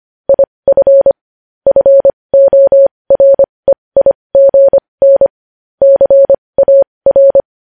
Morse code, heard as a keyed tone, about 25 wpm, spaced as for 20 wpm, 550 Hz. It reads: IF FOREIGN CAR
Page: news headlines in morse code at 20 wpm